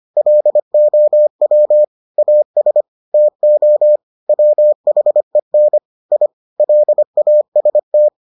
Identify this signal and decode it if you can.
Morse code, heard as a keyed tone, about 25 wpm, 600 Hz. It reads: LOW AS TO WHEN I LAST